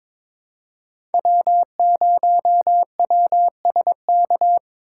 Morse code, heard as a keyed tone, about 22 wpm, 700 Hz. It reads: W0WSK
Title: USA Call Signs - 3x Character Spacing - 22wpm